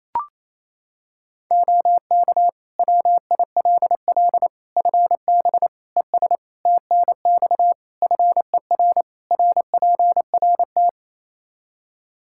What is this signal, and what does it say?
Morse code, heard as a keyed tone, about 28 wpm, 700 Hz. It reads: OK WILL FB ES TNX FER RPRT
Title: Practice QSO - Exchange 2 - Info - 28wpm